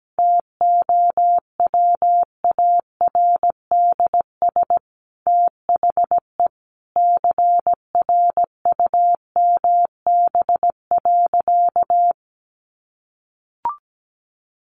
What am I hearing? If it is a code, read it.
Morse code, heard as a keyed tone, about 17 wpm, 700 Hz. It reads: TOWARDS THE CRUMB.
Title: Easy Phrases 17wpm